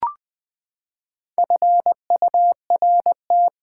Morse code, heard as a keyed tone, about 20 wpm, 700 Hz. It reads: FURT
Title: Build A Word - 4 Letters At A Time - 11-Letter Words - 20wpm